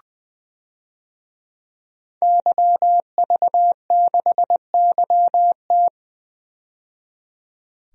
Morse code, heard as a keyed tone, about 20 wpm, 700 Hz. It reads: YV6YT
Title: DXCC-100 Call Signs - 5 Characters - 20wpm